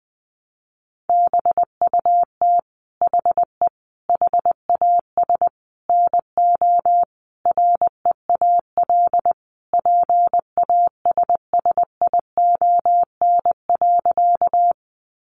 Morse code, heard as a keyed tone, about 19 wpm, 700 Hz. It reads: BUT HE HAS NO REAL PASSION.
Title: Sentences from Top 1300 Words 20wpm